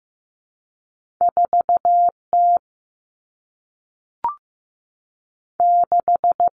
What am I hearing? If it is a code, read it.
Morse code, heard as a keyed tone, about 15 wpm, 700 Hz. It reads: 4 T 6
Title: Common Cut Numbers - 2-Digit 15wpm